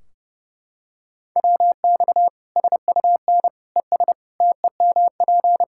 Morse code, heard as a keyed tone, about 32 wpm, 700 Hz. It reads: WX SUN ES TEMP
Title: K1USN SST QSO Practice 30wpm